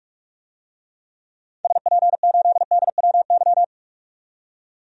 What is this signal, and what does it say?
Morse code, heard as a keyed tone, about 46 wpm, 700 Hz. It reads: SP8DWY
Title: DXCC-100 Call Signs - 6 Characters - 45wpm